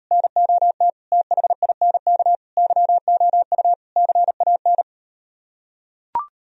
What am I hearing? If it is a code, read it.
Morse code, heard as a keyed tone, about 38 wpm, 700 Hz. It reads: NOT THINK YOU CAN